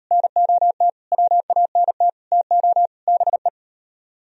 Morse code, heard as a keyed tone, about 38 wpm, 700 Hz. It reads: NOT WANT TO BE